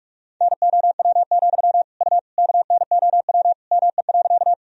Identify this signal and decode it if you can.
Morse code, heard as a keyed tone, about 46 wpm, 700 Hz. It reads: NOW, U KNOW ME.
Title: Sentences from Top 200 Words and Abbreviations 45wpm